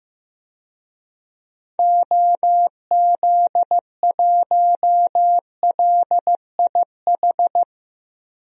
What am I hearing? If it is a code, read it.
Morse code, heard as a keyed tone, about 15 wpm, 700 Hz. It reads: OZ1LIH
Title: DXCC-100 Call Signs - 6 Characters - 15wpm